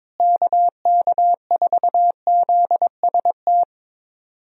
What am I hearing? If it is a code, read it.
Morse code, heard as a keyed tone, about 22 wpm, 700 Hz. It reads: KK4ZST
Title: USA Call Signs - 6 Characters - Repeated 3x - 22wpm